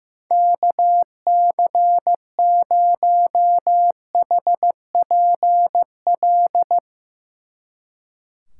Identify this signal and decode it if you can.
Morse code, heard as a keyed tone, about 15 wpm, 700 Hz. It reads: KC0HPL
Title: Speed Racing - USA 2x3 Call Signs 15wpm